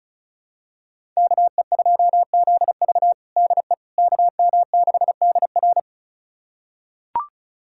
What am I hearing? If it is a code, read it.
Morse code, heard as a keyed tone, about 34 wpm, 700 Hz. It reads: KE2ZV DE KM6DR
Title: USA Callsign DE Callsign - 5 Characters - 35wpm